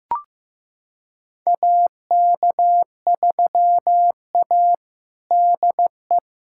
Morse code, heard as a keyed tone, about 15 wpm, 700 Hz. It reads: AK3A DE
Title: USA Callsign DE Callsign - 4 Characters - 15wpm